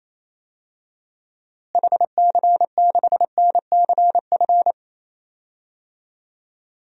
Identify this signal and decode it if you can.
Morse code, heard as a keyed tone, about 28 wpm, 700 Hz. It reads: HC6NCF
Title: DXCC-100 Call Signs - 6 Characters - Repeated 3x - 28wpm